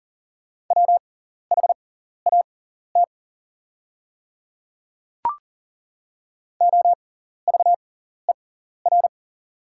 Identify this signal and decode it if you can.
Morse code, heard as a keyed tone, about 40 wpm, spaced as for 14 wpm, 690 Hz. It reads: WHAT OVER
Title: 6x Character Spacing - Top 100 Words - 40wpm